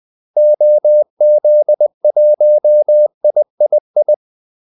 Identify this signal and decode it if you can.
Morse code, heard as a keyed tone, about 20 wpm, 600 Hz. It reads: OZ1III
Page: Kaldesignaler 20 wpm